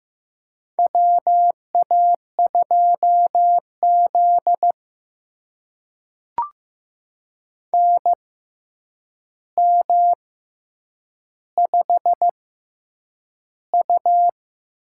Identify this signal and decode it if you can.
Morse code, heard as a keyed tone, about 15 wpm, spaced as for 8 wpm, 700 Hz. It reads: WA2Z NM5U
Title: USA Call Signs - 4 Characters - 6x Character Spacing - 15wpm